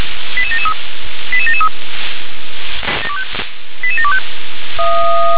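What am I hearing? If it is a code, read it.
Five-tone selective call (ZVEI), heard as quick runs of five tones, tones of about 70 ms. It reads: 88773 88773 88725
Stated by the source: ZVEI five-tone call received by SDR.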